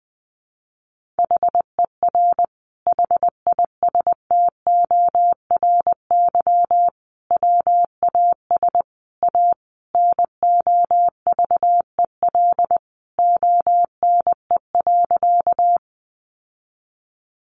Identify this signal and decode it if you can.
Morse code, heard as a keyed tone, about 20 wpm, 700 Hz. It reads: HER HISTORY WAS A NOVEL ONE.
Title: Sentences from Top 1300 Words 20wpm